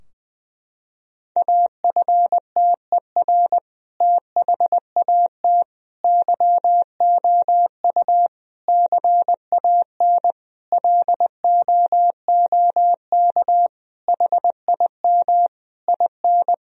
Morse code, heard as a keyed tone, about 20 wpm, 700 Hz. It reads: AFTER THAT YOU CAN LOOK HIM IN
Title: Sentences from Top 1000 Words - Review Part 2 - 20wpm